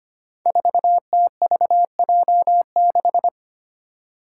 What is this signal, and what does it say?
Morse code, heard as a keyed tone, about 25 wpm, 700 Hz. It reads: 4TVJ6
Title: Five Challenging Letter-Number Combinations 25wpm